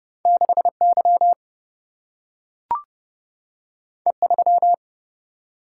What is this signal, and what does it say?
Morse code, heard as a keyed tone, about 30 wpm, 700 Hz. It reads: 6Y E3
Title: Single-Letter Single-Number 30wpm